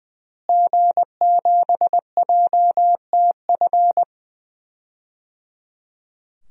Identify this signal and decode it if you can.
Morse code, heard as a keyed tone, about 20 wpm, 700 Hz. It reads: G7JTF
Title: DXCC-100 Call Signs - 5 Characters - Repeated 3x - 20wpm